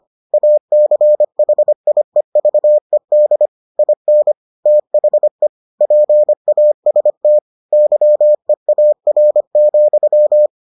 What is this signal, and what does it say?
Morse code, heard as a keyed tone, about 25 wpm, 600 Hz. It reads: ACHIEVED IN THE PAST YEAR,